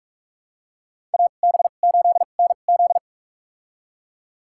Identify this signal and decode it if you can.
Morse code, heard as a keyed tone, about 46 wpm, spaced as for 32 wpm, 700 Hz. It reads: AB8NZ
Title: USA Call Signs - 5 Characters - 2x Character Spacing - 45wpm